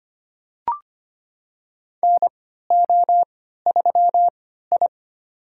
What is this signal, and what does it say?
Morse code, heard as a keyed tone, about 25 wpm, spaced as for 14 wpm, 700 Hz. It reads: NO3I
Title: USA Call Signs - 3x Character Spacing - 25wpm